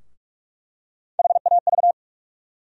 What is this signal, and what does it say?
Morse code, heard as a keyed tone, about 46 wpm, 700 Hz. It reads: HAV
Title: USA Call Sign Postfixes - 3 Letters 45wpm